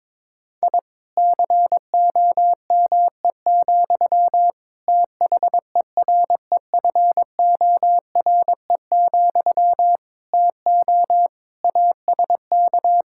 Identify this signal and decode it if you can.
Morse code, heard as a keyed tone, about 22 wpm, 700 Hz. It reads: I COME, THEREFORE, TO ASK
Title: Sentences from Top 1000 Words - Review Part 5 - 22wpm